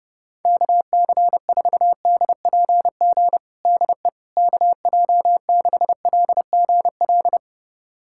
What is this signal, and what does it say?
Morse code, heard as a keyed tone, about 32 wpm, 700 Hz. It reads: KC4DPG DE KJ6LGL
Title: USA Callsign DE Callsign - 6 Characters - 30wpm